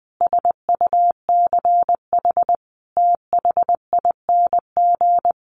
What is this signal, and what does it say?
Morse code, heard as a keyed tone, about 20 wpm, 700 Hz. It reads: SUCH THING
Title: Sets of 5 Words Encore 20wpm